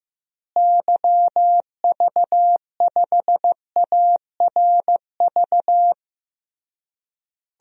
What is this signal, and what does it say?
Morse code, heard as a keyed tone, about 15 wpm, 700 Hz. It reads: YV5ARV
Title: DXCC-100 Call Signs - 6 Characters - Repeated 3x - 15wpm